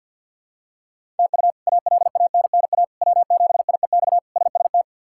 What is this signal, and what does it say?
Morse code, heard as a keyed tone, about 50 wpm, 700 Hz. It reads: TU ALANNA W7IEX SST
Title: K1USN SST QSO Practice - 50wpm